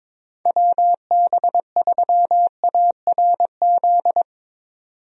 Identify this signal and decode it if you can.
Morse code, heard as a keyed tone, about 22 wpm, 700 Hz. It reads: WB3ARZ
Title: USA Call Signs - 6 Characters - 22wpm